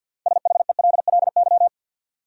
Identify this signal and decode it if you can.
Morse code, heard as a keyed tone, about 53 wpm, 700 Hz. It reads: SHELLY
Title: USA Common Names - 50wpm